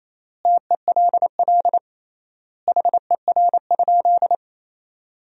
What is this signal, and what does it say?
Morse code, heard as a keyed tone, about 28 wpm, 700 Hz. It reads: TELL HER?
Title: 3x Word-Spacing - Sentences from Top 200 Words - 28wpm